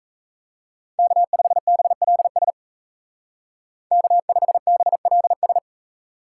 Morse code, heard as a keyed tone, about 34 wpm, 700 Hz. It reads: K5BLS K5BLS
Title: Speed Racing - USA 1x3 Call Signs 28wpm